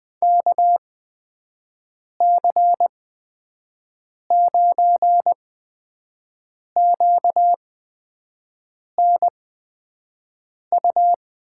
Morse code, heard as a keyed tone, about 20 wpm, spaced as for 5 wpm, 700 Hz. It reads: KC9QNU
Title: USA Call Signs - 6 Characters - 8x Character Spacing - 20wpm